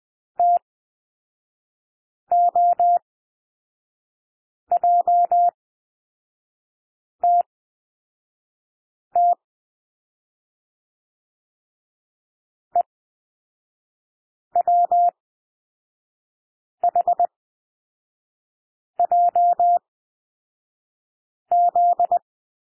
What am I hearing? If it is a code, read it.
Morse code, heard as a keyed tone, about 20 wpm, spaced as for 5 wpm, 700 Hz. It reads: TOJTT EWHJZ
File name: SOUND LETTERS 20 wpm